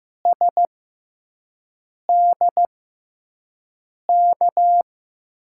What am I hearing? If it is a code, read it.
Morse code, heard as a keyed tone, about 15 wpm, spaced as for 5 wpm, 700 Hz. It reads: SDK
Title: USA Call Signs - 5 Characters - 6x Character Spacing - 15wpm